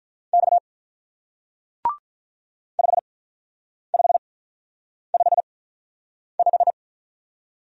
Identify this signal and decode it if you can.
Morse code, heard as a keyed tone, about 43 wpm, spaced as for 9 wpm, 700 Hz. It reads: X 5555
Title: Speed-Racing Single Letter-Number 35wpm